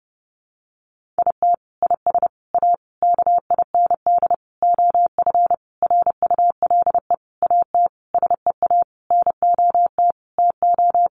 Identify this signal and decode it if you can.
Morse code, heard as a keyed tone, about 28 wpm, 700 Hz. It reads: IT IS A KIND OF RULE AT SEA NOT TO
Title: Sentences from Top 1500 Words 30wpm